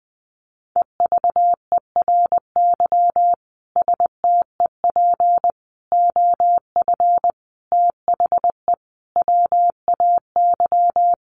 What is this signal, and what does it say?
Morse code, heard as a keyed tone, about 20 wpm, 700 Hz. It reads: EVERY STEP OF THE WAY
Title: Sets of 5 Words Encore2 20wpm